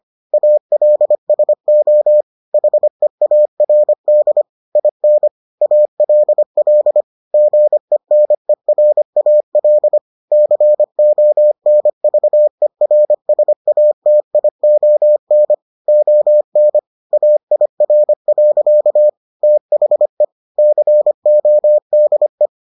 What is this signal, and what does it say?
Morse code, heard as a keyed tone, about 25 wpm, 600 Hz. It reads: ALSO HEARD IN ALL GENERAL CONVERSATION ON AIR. THE CODE